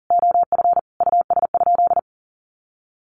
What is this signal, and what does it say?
Morse code, heard as a keyed tone, about 40 wpm, 700 Hz. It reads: OF US?
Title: Sentences from Top 1400 Words 40wpm